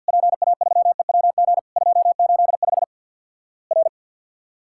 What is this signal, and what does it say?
Morse code, heard as a keyed tone, about 51 wpm, 700 Hz. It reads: PA3EWG 285 R
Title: Worked All Europe DX Contest - QTC Exchange with Repeats - 50wpm